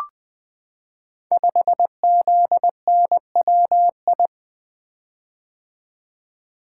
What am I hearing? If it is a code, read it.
Morse code, heard as a keyed tone, about 20 wpm, 700 Hz. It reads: 5ZNWI
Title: Code Groups - 5 Letter:Numbers - 20wpm